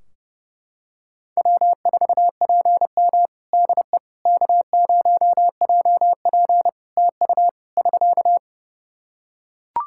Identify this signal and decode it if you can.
Morse code, heard as a keyed tone, about 32 wpm, 700 Hz. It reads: W4PM DE K0JP TU <SK>
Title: Practice QSO - Exchange 4 - Ending - 30wpm